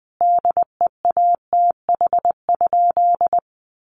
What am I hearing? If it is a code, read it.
Morse code, heard as a keyed tone, about 20 wpm, 700 Hz. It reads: DEATH?
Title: Sentences from Top 1100 Words 20wpm